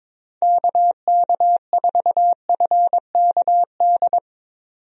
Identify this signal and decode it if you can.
Morse code, heard as a keyed tone, about 22 wpm, 700 Hz. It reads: KK4FKD
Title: USA Call Signs - 6 Characters - 22wpm